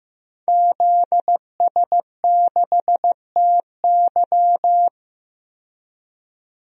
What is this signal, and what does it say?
Morse code, heard as a keyed tone, about 15 wpm, 700 Hz. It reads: ZS6TY
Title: DXCC-100 Call Signs - 5 Characters - 15wpm